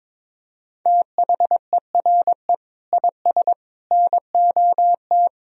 Morse code, heard as a keyed tone, about 21 wpm, 700 Hz. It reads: THERE IS NOT
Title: Sentences from Top 1000 Words - Review Part 4 - 22wpm